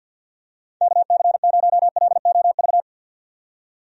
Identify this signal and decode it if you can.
Morse code, heard as a keyed tone, about 46 wpm, 700 Hz. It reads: KC0LOV
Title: USA Call Signs - 6 Characters - 50wpm